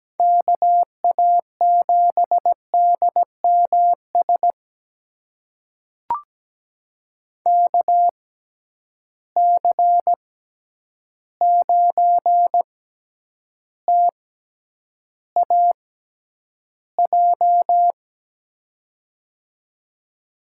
Morse code, heard as a keyed tone, about 17 wpm, spaced as for 9 wpm, 700 Hz. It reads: KA7DMS KC9TAJ
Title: USA Call Signs - 6 Characters - 6x Character Spacing - 17wpm